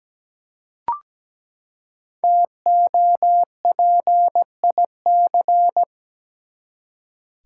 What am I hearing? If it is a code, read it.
Morse code, heard as a keyed tone, about 17 wpm, 700 Hz. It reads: TOPIC
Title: Five Letter Words - 17wpm